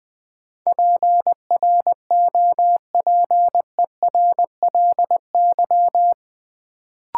Morse code, heard as a keyed tone, about 20 wpm, 700 Hz. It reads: PROPERLY
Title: Sentences from Top 1100 Words - New Word Drill Alt - 20wpm